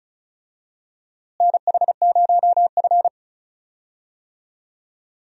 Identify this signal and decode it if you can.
Morse code, heard as a keyed tone, about 36 wpm, 700 Hz. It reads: NH0F
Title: USA Call Signs - 4 Characters - Repeated 2x - 35wpm